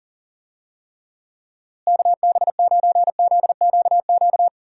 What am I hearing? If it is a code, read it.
Morse code, heard as a keyed tone, about 40 wpm, 700 Hz. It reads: KB9ZQQ
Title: USA Call Signs - 6 Characters - Repeated 2x - 40wpm